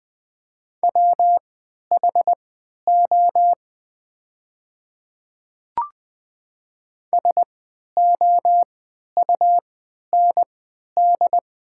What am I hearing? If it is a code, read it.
Morse code, heard as a keyed tone, about 20 wpm, spaced as for 11 wpm, 700 Hz. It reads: WHO SOUND
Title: 3x Character Spacing - Top 300 Words - 20wpm